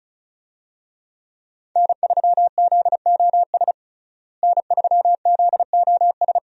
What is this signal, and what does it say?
Morse code, heard as a keyed tone, about 34 wpm, 700 Hz. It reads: N3ZOS N3ZOS
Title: USA Call Signs - 5 Characters - Repeated 2x - 35wpm